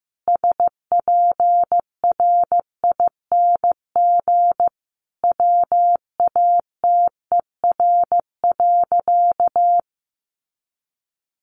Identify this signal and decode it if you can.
Morse code, heard as a keyed tone, about 15 wpm, 700 Hz. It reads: SPRING WATER.
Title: Sentences from Top 1200 Words 15wpm